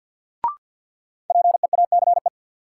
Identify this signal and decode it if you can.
Morse code, heard as a keyed tone, about 53 wpm, 700 Hz. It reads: PEACE
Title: Sentences from Top 1500 Words - New Word Drill Alt - 50wpm